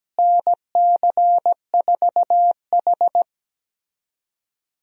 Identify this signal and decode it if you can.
Morse code, heard as a keyed tone, about 17 wpm, 700 Hz. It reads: NC4H
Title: USA Call Signs - 4 Characters - Repeated 3x - 17wpm